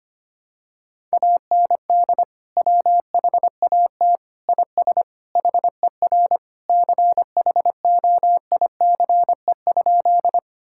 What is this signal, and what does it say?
Morse code, heard as a keyed tone, about 25 wpm, 700 Hz. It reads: AND WHAT IS HER CHOICE?
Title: Sentences from Top 1000 Words - Review Part 6 - 25wpm